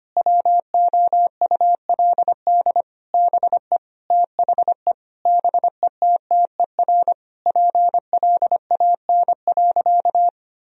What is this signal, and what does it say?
Morse code, heard as a keyed tone, about 25 wpm, 700 Hz. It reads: WOULD BE THE BETTER PLAN.
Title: Sentences from Top 1000 Words - Review Part 2 - 25wpm